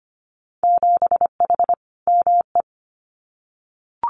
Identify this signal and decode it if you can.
Morse code, heard as a keyed tone, about 25 wpm, 690 Hz. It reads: 7H ME
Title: Winter Field Day Exchanges Encore 25wpm